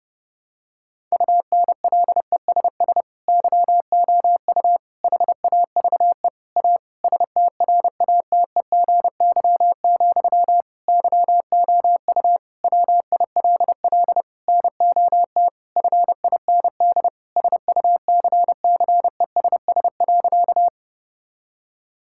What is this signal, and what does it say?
Morse code, heard as a keyed tone, about 32 wpm, 700 Hz. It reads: UNLESS YOU HAVE A STRATEGY, YOU WILL NOT FIND SUCCESS.
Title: Sentences from Top 1000 Words - Review Part 5 - 30wpm